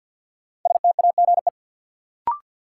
Morse code, heard as a keyed tone, about 48 wpm, 700 Hz. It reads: STAGE
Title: Sentences from Top 1200 Words - New Word Drill - 50wpm